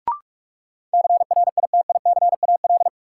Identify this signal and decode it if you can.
Morse code, heard as a keyed tone, about 46 wpm, 700 Hz. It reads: CRITICAL
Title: Sentences from Top 1100 Words - New Word Drill Alt - 45wpm